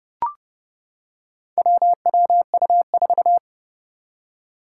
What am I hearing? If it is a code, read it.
Morse code, heard as a keyed tone, about 30 wpm, 690 Hz. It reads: WWU4
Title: Four Challenging Letter-Number Combinations 30wpm